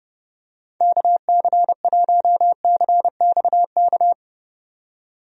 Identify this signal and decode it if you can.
Morse code, heard as a keyed tone, about 30 wpm, 700 Hz. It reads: KC1CXK
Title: USA Call Signs - 6 Characters - 30wpm